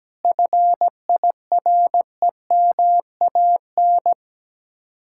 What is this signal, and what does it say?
Morse code, heard as a keyed tone, about 17 wpm, 700 Hz. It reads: FIREMAN
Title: Speed-Racing Common QSO Phrases - 17wpm